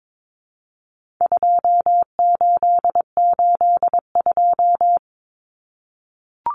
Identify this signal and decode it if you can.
Morse code, heard as a keyed tone, about 22 wpm, 700 Hz. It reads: 2882
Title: Common Cut Numbers - 4-Digit 22wpm